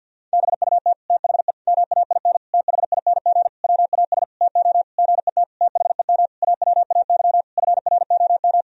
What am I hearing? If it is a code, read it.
Morse code, heard as a keyed tone, about 52 wpm, 700 Hz. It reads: BUT THE MAIN THING WAS TO GET THEM AWAY FROM
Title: Sentences from Top 1000 Words - Review Part 2 - 50wpm